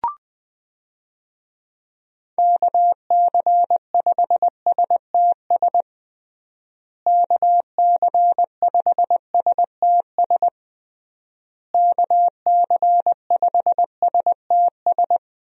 Morse code, heard as a keyed tone, about 20 wpm, 700 Hz. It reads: KC5STS KC5STS KC5STS
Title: USA Call Signs - 6 Characters - Repeated 3x - 20wpm